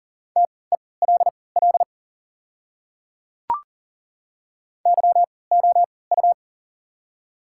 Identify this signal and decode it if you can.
Morse code, heard as a keyed tone, about 40 wpm, spaced as for 23 wpm, 700 Hz. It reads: TELL YOU
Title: 3x Character Spacing - Top 500 Words - 40wpm